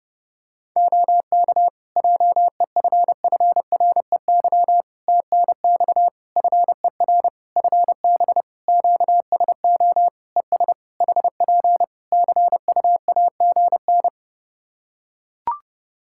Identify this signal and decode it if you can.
Morse code, heard as a keyed tone, about 30 wpm, 700 Hz. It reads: OK JEFFREY TNX FER FB QSO ES HP CUAGN
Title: Practice QSO - Exchange 4 - Ending - 30wpm